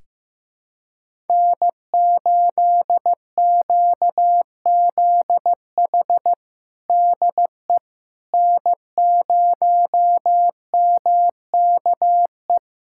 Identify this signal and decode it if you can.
Morse code, heard as a keyed tone, about 15 wpm, 700 Hz. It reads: N8QZH DE N0MKE
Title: USA Callsign DE Callsign - 5 Characters - 15wpm